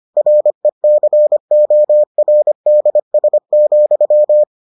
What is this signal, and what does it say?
Morse code, heard as a keyed tone, about 25 wpm, 600 Hz. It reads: RECORDS,